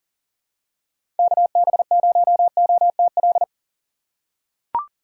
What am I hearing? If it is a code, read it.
Morse code, heard as a keyed tone, about 39 wpm, 700 Hz. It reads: KB0OTL
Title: USA Call Signs - 6 Characters - 3x Character Spacing - 40wpm